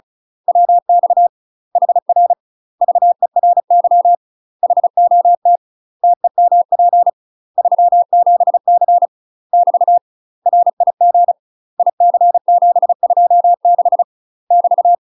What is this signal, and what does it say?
Morse code, heard as a keyed tone, about 34 wpm, 700 Hz. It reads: WX HR VERY HOT TEMP 37C <BT> RIG IC726 <BT>
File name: UK MASTER QSOs TRAINING / 35WPM / UK MASTER QSOs 0037